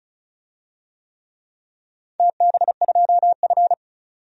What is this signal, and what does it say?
Morse code, heard as a keyed tone, about 36 wpm, 700 Hz. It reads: TB2F
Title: DXCC-100 Call Signs - 4 Characters - 35wpm